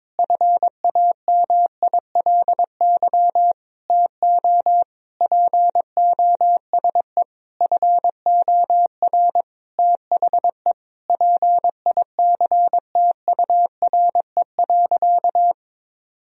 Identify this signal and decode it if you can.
Morse code, heard as a keyed tone, about 22 wpm, 700 Hz. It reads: FAMILY TO POSE FOR THE PICTURE.
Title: Sentences from Top 1000 Words - Review Part 2 - 22wpm